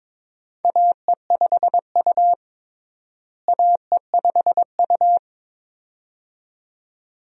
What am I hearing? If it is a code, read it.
Morse code, heard as a keyed tone, about 22 wpm, 700 Hz. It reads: AE5U AE5U
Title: USA Call Signs - 4 Characters - Repeated 2x - 22wpm